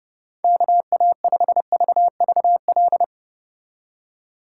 Morse code, heard as a keyed tone, about 30 wpm, 700 Hz. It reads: KA5VVL
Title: Speed Racing - USA Call Signs 25wpm